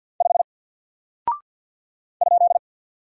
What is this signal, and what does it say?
Morse code, heard as a keyed tone, about 48 wpm, 700 Hz. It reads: 5 ?
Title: Single Character (Letter:Number:Punctuation:Prosign) Alt 50wpm